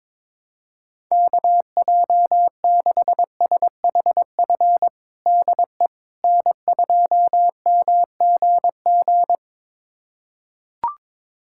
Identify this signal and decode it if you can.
Morse code, heard as a keyed tone, about 22 wpm, 700 Hz. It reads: KJ6SHF DE N2MGG
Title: USA Callsign DE Callsign 22wpm